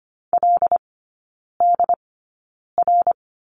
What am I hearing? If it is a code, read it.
Morse code, heard as a keyed tone, about 25 wpm, spaced as for 9 wpm, 700 Hz. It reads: LDR
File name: Lesson 17e - Letter u - Three Characters 25wpm